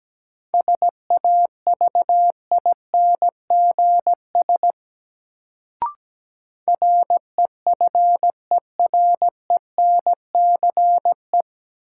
Morse code, heard as a keyed tone, about 17 wpm, 700 Hz. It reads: SAVINGS REFERENCE
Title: Sentences from Top 1200 Words - New Word Drill - 17wpm